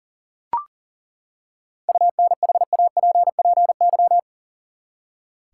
Morse code, heard as a keyed tone, about 40 wpm, 700 Hz. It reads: UNHAPPY
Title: Seven Letter Words - 40wpm